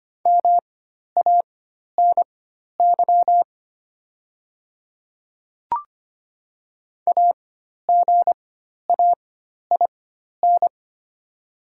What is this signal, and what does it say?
Morse code, heard as a keyed tone, about 25 wpm, spaced as for 12 wpm, 700 Hz. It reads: MANY AGAIN
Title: 4x Character Spacing - Top 200 Words - 25wpm